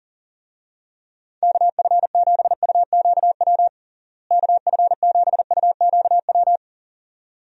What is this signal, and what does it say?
Morse code, heard as a keyed tone, about 40 wpm, 700 Hz. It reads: KF7UQW KF7UQW
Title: USA Call Signs - 6 Characters - Repeated 2x - 40wpm